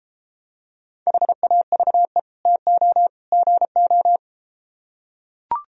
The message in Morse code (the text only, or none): HAVE TO GO